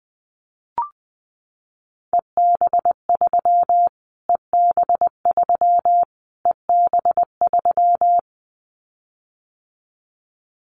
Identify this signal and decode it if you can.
Morse code, heard as a keyed tone, about 20 wpm, 700 Hz. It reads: EB3 EB3 EB3
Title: Callsign Prefixes for Top 25 DXCC - Repeated 3x - 20wpm